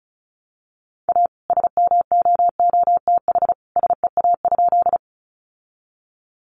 Morse code, heard as a keyed tone, about 34 wpm, 700 Hz. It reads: A SMOOTH SEA?
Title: Sentences from Top 1400 Words 35wpm